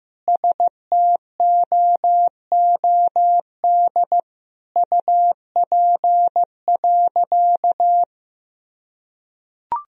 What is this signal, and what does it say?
Morse code, heard as a keyed tone, about 15 wpm, 700 Hz. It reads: STOOD UP.
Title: Sentences from Top 1000 Words - Review Part 2 - 15wpm